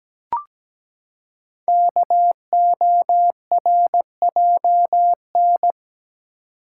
Morse code, heard as a keyed tone, about 17 wpm, 700 Hz. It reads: KORJN
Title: Code Groups - 5 Letters - 17wpm